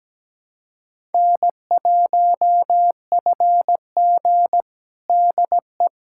Morse code, heard as a keyed tone, about 17 wpm, 700 Hz. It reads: N1FG DE
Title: USA Callsign DE Callsign - 4 Characters - 17wpm